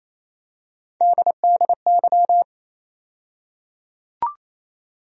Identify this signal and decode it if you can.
Morse code, heard as a keyed tone, about 28 wpm, 700 Hz. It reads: DDY
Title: USA Call Sign Postfixes - 3 Letters 28wpm